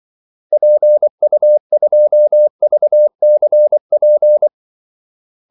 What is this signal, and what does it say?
Morse code, heard as a keyed tone, about 24 wpm, 600 Hz. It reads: PU2VCP
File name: Callsigns-24wpm